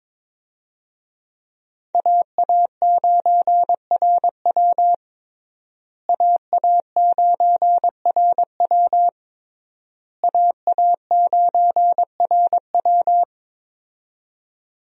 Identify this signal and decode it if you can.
Morse code, heard as a keyed tone, about 22 wpm, 700 Hz. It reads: AA9RW AA9RW AA9RW
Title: USA Call Signs - 5 Characters - Repeated 3x - 22wpm